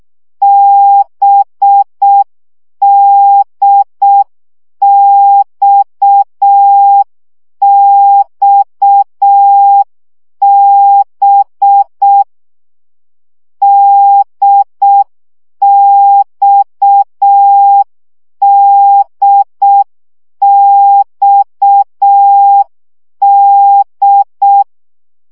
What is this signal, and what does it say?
Morse code, heard as a keyed tone, about 6 wpm, 800 Hz. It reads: BDXXB DXDXD
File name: CW gyakorlas d b x 5 baud